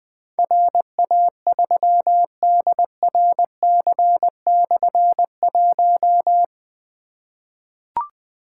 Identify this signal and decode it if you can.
Morse code, heard as a keyed tone, about 20 wpm, 700 Hz. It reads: RA3DRC/1
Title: Difficult Call Signs 20wpm